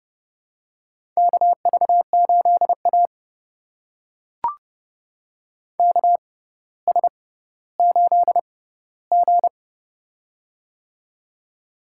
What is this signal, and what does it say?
Morse code, heard as a keyed tone, about 30 wpm, spaced as for 15 wpm, 700 Hz. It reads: KV8A KS8G
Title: USA Call Signs - 4 Characters - 6x Character Spacing - 30wpm